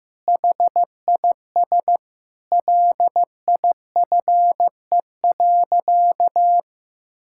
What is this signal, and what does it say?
Morse code, heard as a keyed tone, about 15 wpm, 700 Hz. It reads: HIS LIFE.
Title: Sentences from Top 1000 Words - Review Part 5 - 15wpm